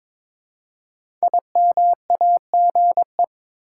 Morse code, heard as a keyed tone, about 22 wpm, 690 Hz. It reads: IMAGE
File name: 5 Letter Words - First 3 Letters Given - 22wpm